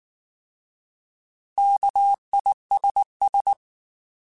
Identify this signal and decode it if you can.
Morse code, heard as a keyed tone, about 19 wpm, 800 Hz. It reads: KISS
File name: Alphabetical / Alphabetical mp3s / Ks @18